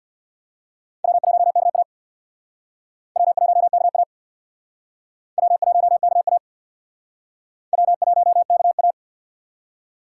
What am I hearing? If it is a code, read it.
Morse code, heard as a keyed tone, about 66 wpm, 700 Hz. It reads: W1KA W1KA W1KA W1KA